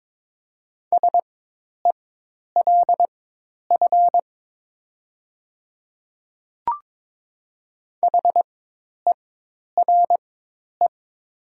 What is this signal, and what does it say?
Morse code, heard as a keyed tone, about 22 wpm, spaced as for 10 wpm, 690 Hz. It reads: SELF HERE
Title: 4x Character Spacing - Top 200 Words - 22wpm